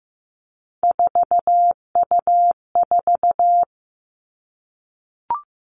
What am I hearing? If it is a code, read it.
Morse code, heard as a keyed tone, about 15 wpm, 700 Hz. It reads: 4U4
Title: Three Challenging Letter-Number Combinations 15wpm